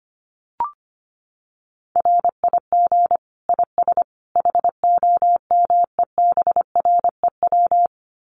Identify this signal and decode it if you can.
Morse code, heard as a keyed tone, about 25 wpm, 700 Hz. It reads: RIG IS HOMEBREW
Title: Common QSO Phrases 25wpm